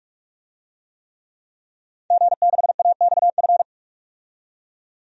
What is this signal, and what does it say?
Morse code, heard as a keyed tone, about 41 wpm, 700 Hz. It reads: G6AXF
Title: DXCC-100 Call Signs - 5 Characters - 45wpm